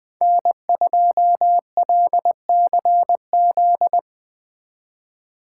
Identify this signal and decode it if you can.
Morse code, heard as a keyed tone, about 20 wpm, 700 Hz. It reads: N2LCZ
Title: Speed Racing - USA Call Signs 17wpm